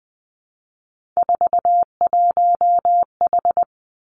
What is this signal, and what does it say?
Morse code, heard as a keyed tone, about 19 wpm, 700 Hz. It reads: 41H
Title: Three Challenging Letter-Number Combinations 20wpm